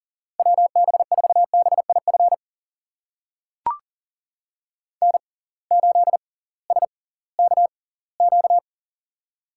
Morse code, heard as a keyed tone, about 39 wpm, spaced as for 21 wpm, 700 Hz. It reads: WB4BIF N8SKQ
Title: USA Call Signs - 6x Character Spacing - 40wpm